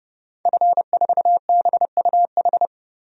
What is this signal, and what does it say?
Morse code, heard as a keyed tone, about 31 wpm, 700 Hz. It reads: F4BUH
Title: DXCC-100 Call Signs - 5 Characters - Repeated 3x - 30wpm